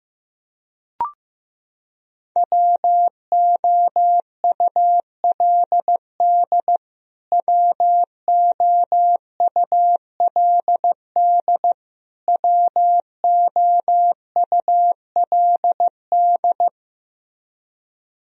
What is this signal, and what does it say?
Morse code, heard as a keyed tone, about 15 wpm, 700 Hz. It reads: WOULD WOULD WOULD
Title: Top 100 Words Repeated 15wpm